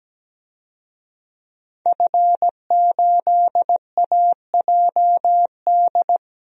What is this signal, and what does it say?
Morse code, heard as a keyed tone, about 17 wpm, 700 Hz. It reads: F8AJD
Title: DXCC-100 Call Signs - 5 Characters - 17wpm